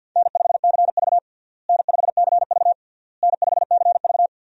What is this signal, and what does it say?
Morse code, heard as a keyed tone, about 52 wpm, 700 Hz. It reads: N5CV N5CV N5CV
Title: USA Call Signs - 4 Characters - Repeated 3x - 50wpm